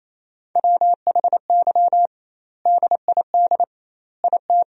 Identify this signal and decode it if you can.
Morse code, heard as a keyed tone, about 28 wpm, 700 Hz. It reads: WHY DID IT
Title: 2x Word-Spacing - Sentences from Top 400 Words - 28wpm